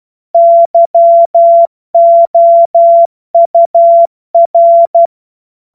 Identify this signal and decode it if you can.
Morse code, heard as a keyed tone, about 12 wpm, 680 Hz. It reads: YOUR
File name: Morsekurs / Witze